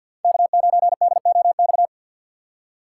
Morse code, heard as a keyed tone, about 51 wpm, 700 Hz. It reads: K9DOX
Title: USA Call Signs - 5 Characters - 4x Character Spacing - 50wpm